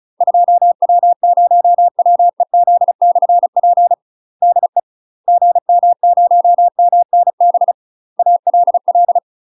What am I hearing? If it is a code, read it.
Morse code, heard as a keyed tone, about 35 wpm, 700 Hz. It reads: 2W0WEZ/P DE GM0MNB ALL
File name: UK MASTER QSOs TRAINING / 35WPM / UK MASTER QSOs 0039